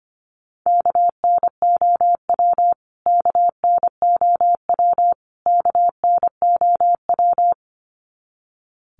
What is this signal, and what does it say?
Morse code, heard as a keyed tone, about 25 wpm, 700 Hz. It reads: KNOW KNOW KNOW
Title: Top 500 Words Repeated 25wpm